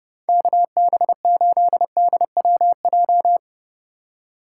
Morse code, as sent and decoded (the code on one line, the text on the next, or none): -.- -... ---.. -.. .-- .---
KB8DWJ